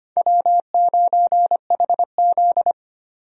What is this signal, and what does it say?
Morse code, heard as a keyed tone, about 25 wpm, 700 Hz. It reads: W9HZ
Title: USA Call Signs - 4 Characters - 25wpm